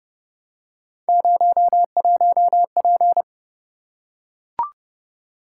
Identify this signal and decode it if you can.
Morse code, heard as a keyed tone, about 29 wpm, 700 Hz. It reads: 01P